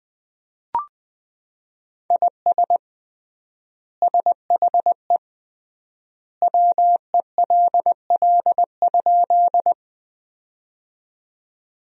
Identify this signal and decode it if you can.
Morse code, heard as a keyed tone, about 20 wpm, 700 Hz. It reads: IS SHE WELL?
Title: 3x Word-Spacing - Sentences from Top 200 Words - 20wpm